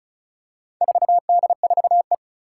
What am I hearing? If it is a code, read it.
Morse code, heard as a keyed tone, about 36 wpm, 700 Hz. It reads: 4D4E
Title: Four Challenging Letter-Number Combinations 35wpm